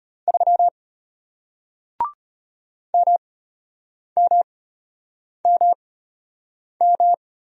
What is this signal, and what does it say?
Morse code, heard as a keyed tone, about 33 wpm, spaced as for 8 wpm, 700 Hz. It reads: 3 MMMM